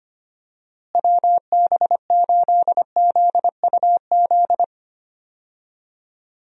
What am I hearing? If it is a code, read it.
Morse code, heard as a keyed tone, about 25 wpm, 700 Hz. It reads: WB8ZUZ
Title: USA Call Signs - 6 Characters - Repeated 2x - 25wpm